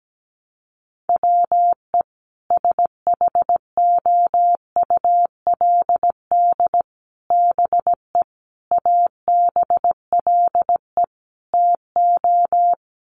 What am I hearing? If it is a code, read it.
Morse code, heard as a keyed tone, about 17 wpm, 700 Hz. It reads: WE SHOULD BE ABLE TO
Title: Sets of 5 Words Encore2 17wpm